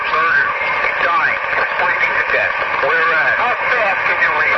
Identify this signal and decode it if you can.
Emergency Alert System (SAME) tones, heard as harsh buzzing bursts, with speech. It is heard from 0.2 to 3.4 s.